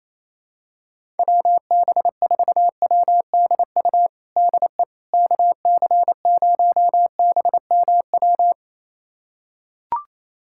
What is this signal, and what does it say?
Morse code, heard as a keyed tone, about 28 wpm, 700 Hz. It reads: WB4WDU DE KC0BMW
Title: USA Callsign DE Callsign - 6 Characters - 28wpm